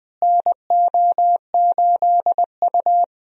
Morse code, heard as a keyed tone, about 20 wpm, 700 Hz. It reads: NO8U
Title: USA Call Signs - 4 Characters - Repeated 3x - 20wpm